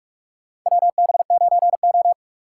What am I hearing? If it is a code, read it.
Morse code, heard as a keyed tone, about 46 wpm, 700 Hz. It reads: WB9O
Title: USA Call Signs - 4 Characters - 6x Character Spacing - 45wpm